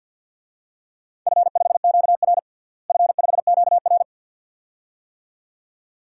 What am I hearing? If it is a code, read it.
Morse code, heard as a keyed tone, about 51 wpm, 700 Hz. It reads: F5QR F5QR
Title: DXCC-100 Call Signs - 4 Characters - Repeated 2x - 50wpm